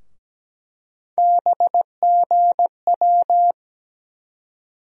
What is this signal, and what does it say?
Morse code, heard as a keyed tone, about 17 wpm, 700 Hz. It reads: BGW
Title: USA Call Sign Postfixes - 3 Letters 17wpm